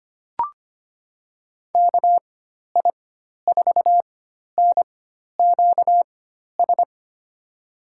Morse code, heard as a keyed tone, about 25 wpm, spaced as for 12 wpm, 700 Hz. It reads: KI4NQS
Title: USA Call Signs - 6 Characters - 4x Character Spacing - 25wpm